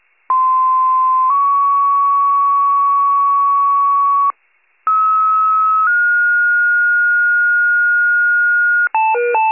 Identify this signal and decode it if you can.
Two-tone page, heard as a one-second tone, then a three-second tone, about 1040 then 1120 Hz, about 1290 then 1430 Hz. It is heard from 0.3 to 8.9 s.